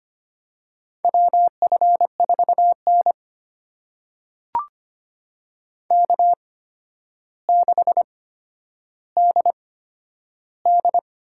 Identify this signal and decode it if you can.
Morse code, heard as a keyed tone, about 25 wpm, spaced as for 11 wpm, 700 Hz. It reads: WF4N K6DD
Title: USA Call Signs - 4 Characters - 8x Character Spacing - 25wpm